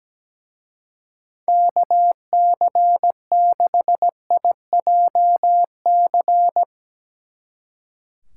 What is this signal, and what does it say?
Morse code, heard as a keyed tone, about 17 wpm, 700 Hz. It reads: KC6IJC
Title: Speed Racing - USA Call Signs 17wpm